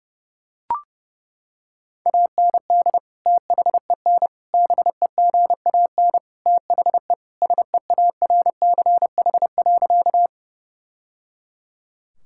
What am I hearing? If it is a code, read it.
Morse code, heard as a keyed tone, about 30 wpm, 700 Hz. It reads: AND THEN BEGAN THE SEARCH.
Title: Sentences from Top 1000 Words - Review Part 5 - 30wpm